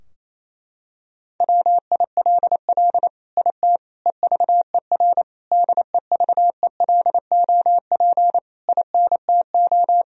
Morse code, heard as a keyed tone, about 28 wpm, 700 Hz. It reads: WILL IT EVER DEVELOP INTO
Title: Sentences from Top 1000 Words - Review Part 2 - 28wpm